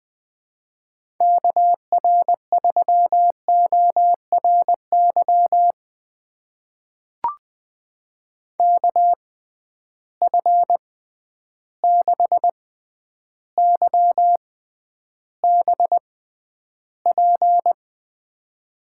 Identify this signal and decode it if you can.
Morse code, heard as a keyed tone, about 20 wpm, spaced as for 10 wpm, 700 Hz. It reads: KR3ORY KF6YBP
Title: USA Call Signs - 6 Characters - 6x Character Spacing - 20wpm